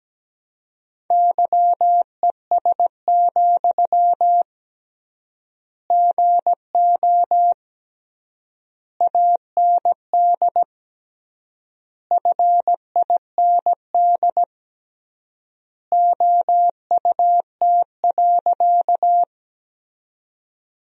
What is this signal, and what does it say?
Morse code, heard as a keyed tone, about 17 wpm, 700 Hz. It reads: YES, GO AND FIND OUT.
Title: 3x Word-Spacing - Sentences from Top 500 Words - 17wpm